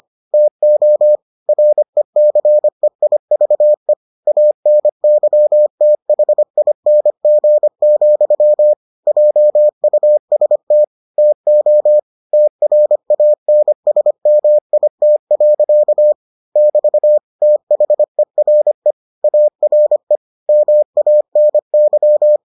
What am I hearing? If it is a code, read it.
Morse code, heard as a keyed tone, about 25 wpm, 600 Hz. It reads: TO RECEIVE ANYTHING, JUST TO TRANSMIT. <BT> THERE ARE MANY